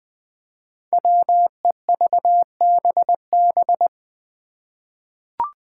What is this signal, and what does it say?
Morse code, heard as a keyed tone, about 21 wpm, 700 Hz. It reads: WEVBB
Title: Five Challenging Letter-Number Combinations 20wpm